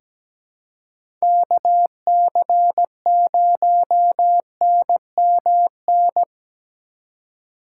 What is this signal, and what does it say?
Morse code, heard as a keyed tone, about 17 wpm, 700 Hz. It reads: KC0NMN
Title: USA Call Signs - 3x Character Spacing - 17wpm